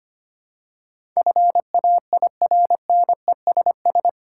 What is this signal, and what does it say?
Morse code, heard as a keyed tone, about 25 wpm, 700 Hz. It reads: FAIRNESS
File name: Eight Letter Words - 25wpm